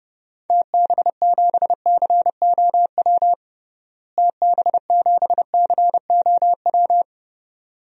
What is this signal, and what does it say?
Morse code, heard as a keyed tone, about 30 wpm, 700 Hz. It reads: TB7COW TB7COW
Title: DXCC-100 Call Signs - 6 Characters - Repeated 2x - 30wpm